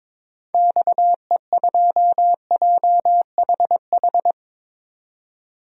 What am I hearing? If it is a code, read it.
Morse code, heard as a keyed tone, about 21 wpm, 700 Hz. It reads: XE2JHH
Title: DXCC-100 Call Signs - 6 Characters - 22wpm